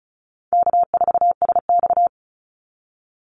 Morse code, heard as a keyed tone, about 34 wpm, 700 Hz. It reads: K4SX
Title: Call Signs - 35wpm